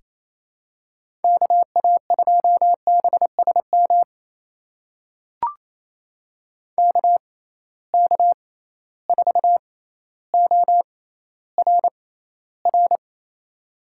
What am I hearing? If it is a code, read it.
Morse code, heard as a keyed tone, about 27 wpm, spaced as for 14 wpm, 700 Hz. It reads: KA2BSM KK4ORR
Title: USA Call Signs - 6 Characters - 6x Character Spacing - 28wpm